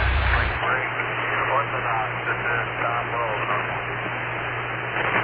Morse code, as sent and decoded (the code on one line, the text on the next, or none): none